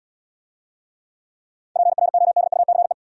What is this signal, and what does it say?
Morse code, heard as a keyed tone, about 74 wpm, 700 Hz. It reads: PRODUCE